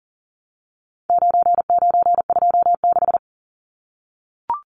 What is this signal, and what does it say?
Morse code, heard as a keyed tone, about 40 wpm, 700 Hz. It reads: 9926
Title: Numbers - 4-Digit 40wpm